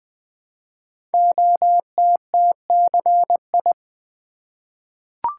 Morse code, heard as a keyed tone, about 20 wpm, 700 Hz. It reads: OTTCI
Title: Code Groups - 5 Letters - 20wpm